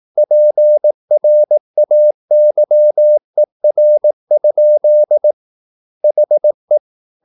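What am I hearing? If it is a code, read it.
Morse code, heard as a keyed tone, about 18 wpm, 600 Hz. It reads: PRAYER? HE
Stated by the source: Morse practice at 18 wpm, English text, from CW med Gnister.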